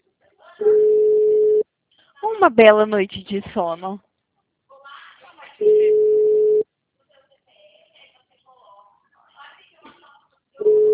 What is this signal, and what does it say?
Ringback tone (Europe, 425 Hz), with speech.